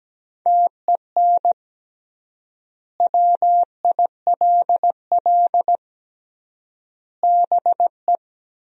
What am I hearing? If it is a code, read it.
Morse code, heard as a keyed tone, about 17 wpm, 700 Hz. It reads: TEN WILL BE
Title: 3x Word-Spacing - Sentences from Top 400 Words - 17wpm